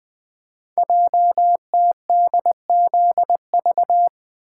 Morse code, heard as a keyed tone, about 20 wpm, 700 Hz. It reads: JTDZV
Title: Code Groups - 5 Letters - 20wpm